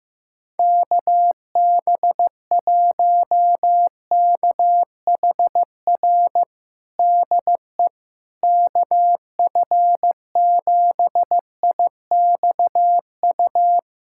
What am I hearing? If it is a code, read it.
Morse code, heard as a keyed tone, about 15 wpm, 700 Hz. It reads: KB1KHR DE KF7IXU
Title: USA Callsign DE Callsign - 6 Characters - 15wpm